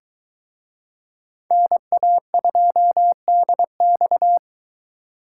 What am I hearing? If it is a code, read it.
Morse code, heard as a keyed tone, about 22 wpm, 700 Hz. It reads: NA2DX